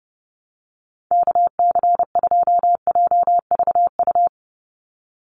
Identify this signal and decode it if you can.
Morse code, heard as a keyed tone, about 29 wpm, 700 Hz. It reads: KC2JVU
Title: USA Call Signs 30wpm